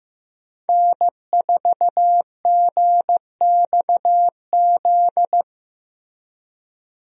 Morse code, heard as a keyed tone, about 15 wpm, 700 Hz. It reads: N4GXZ
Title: USA Call Signs - 5 Characters - Repeated 3x - 15wpm